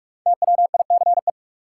Morse code, heard as a keyed tone, about 43 wpm, 700 Hz. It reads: TWICE
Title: Five Letter Words - 45wpm